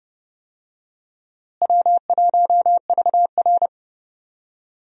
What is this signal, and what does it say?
Morse code, heard as a keyed tone, about 30 wpm, 700 Hz. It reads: W1VR
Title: USA Call Signs - 4 Characters - Repeated 2x - 30wpm